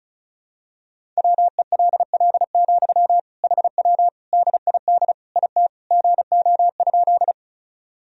Morse code, heard as a keyed tone, about 34 wpm, 700 Hz. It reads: WELL, HW DID IT GO?
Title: Sentences from Top 200 Words and Abbreviations 35wpm